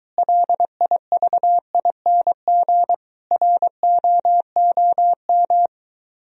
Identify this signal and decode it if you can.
Morse code, heard as a keyed tone, about 23 wpm, 700 Hz. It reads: LIVING ROOM